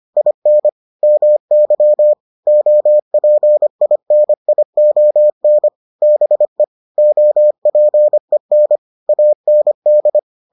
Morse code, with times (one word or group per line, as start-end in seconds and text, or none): 0.2-0.7 IN
1.0-2.1 MY
2.5-5.7 OPINION
6.0-6.6 BE
7.0-8.8 OPEN
9.1-10.2 AND